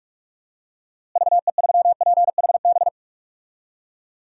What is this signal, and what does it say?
Morse code, heard as a keyed tone, about 46 wpm, 700 Hz. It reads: VE3PHB
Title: DXCC-100 Call Signs - 6 Characters - Repeated 3x - 45wpm